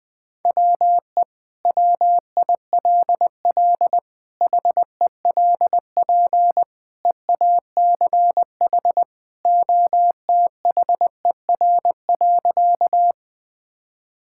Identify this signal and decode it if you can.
Morse code, heard as a keyed tone, about 19 wpm, 700 Hz. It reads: WE WILL HELP EACH OTHER.
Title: Sentences with 2-Word Phrases - 20wpm